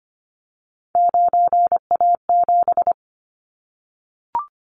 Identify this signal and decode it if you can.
Morse code, heard as a keyed tone, about 25 wpm, 700 Hz. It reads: 9A7
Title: Callsign Prefixes for Top 25 DXCC - 25wpm